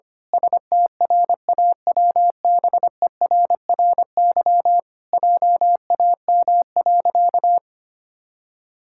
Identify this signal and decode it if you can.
Morse code, heard as a keyed tone, about 25 wpm, 700 Hz. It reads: STRAWBERRY JAM.
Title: Pangram Challenge v2 - 25wpm